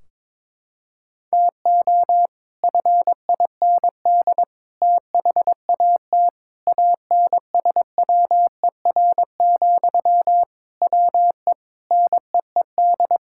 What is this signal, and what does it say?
Morse code, heard as a keyed tone, about 22 wpm, 700 Hz. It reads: TO FIND THAT ANSWER, WE NEED
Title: Sentences from Top 1600 Words 22wpm